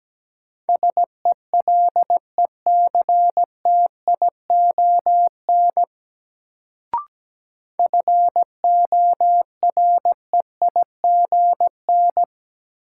Morse code, heard as a keyed tone, about 17 wpm, 700 Hz. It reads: SELECTION FOREIGN
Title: Sentences from Top 1400 Words - New Word Drill - 17wpm